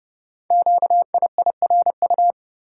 Morse code, heard as a keyed tone, about 30 wpm, 700 Hz. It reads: QIIRU
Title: Code Groups - 5 Letters - 30wpm